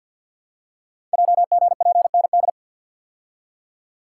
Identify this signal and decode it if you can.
Morse code, heard as a keyed tone, about 51 wpm, 700 Hz. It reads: JGPND
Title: Code Groups - 5 Letters - 50wpm